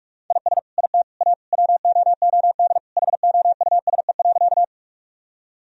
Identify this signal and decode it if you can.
Morse code, heard as a keyed tone, about 45 wpm, 700 Hz. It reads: IS IT A WOOD HOUSE.
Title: Sentences from Top 1000 Words - Review Part 2 - 45wpm